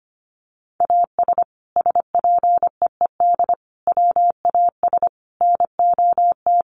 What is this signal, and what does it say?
Morse code, heard as a keyed tone, about 25 wpm, 700 Hz. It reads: AS SPEED WAS NOT
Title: Sentences from Top 800 Words 25wpm